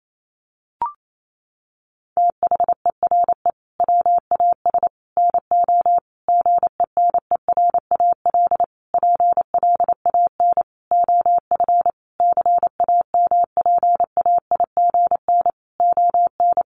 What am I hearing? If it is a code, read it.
Morse code, heard as a keyed tone, about 28 wpm, 700 Hz. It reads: THERE WAS NO GENERAL PLAN OF CAMPAIGN ON
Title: Sentences from Top 1200 Words 28wpm